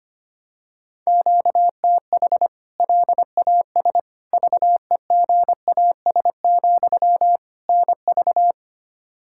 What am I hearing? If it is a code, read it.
Morse code, heard as a keyed tone, about 25 wpm, 700 Hz. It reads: QTH LAS VEGAS, NV
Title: K1USN SST QSO Practice 25wpm